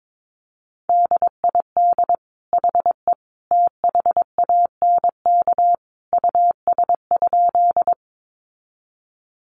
Morse code, heard as a keyed tone, about 22 wpm, 700 Hz. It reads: DID HE THANK US?
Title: Sentences from Top 600 Words 22wpm